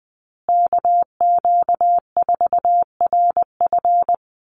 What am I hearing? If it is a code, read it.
Morse code, heard as a keyed tone, about 20 wpm, 700 Hz. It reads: KQ4RF
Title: USA Call Signs Repeated 3x 20wpm